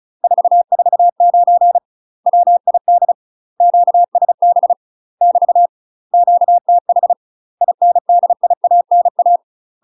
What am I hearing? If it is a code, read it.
Morse code, heard as a keyed tone, about 36 wpm, 700 Hz. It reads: 449 WID QSB <BT> QTH INDIANA
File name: UK MASTER QSOs TRAINING / 35WPM / UK MASTER QSOs 0036